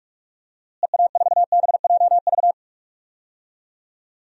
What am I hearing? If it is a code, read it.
Morse code, heard as a keyed tone, about 45 wpm, 700 Hz. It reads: EA4BJV
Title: DXCC-100 Call Signs - 6 Characters - Repeated 3x - 45wpm